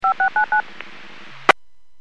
DTMF, heard as telephone touch tones, tones of about 110 ms. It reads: 5699